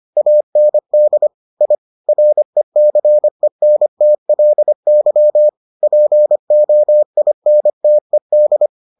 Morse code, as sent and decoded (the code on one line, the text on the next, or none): .- -. -.. / .. / .-. -.-. -. - .-.. -.-- / .--. --- .. -. - . -..
AND I RECENTLY POINTED